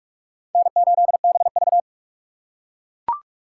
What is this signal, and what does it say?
Morse code, heard as a keyed tone, about 38 wpm, 700 Hz. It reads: N8BV